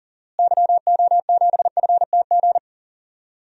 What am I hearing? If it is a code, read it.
Morse code, heard as a keyed tone, about 40 wpm, 700 Hz. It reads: YO7FTG
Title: DXCC-100 Call Signs - 6 Characters - Repeated 3x - 40wpm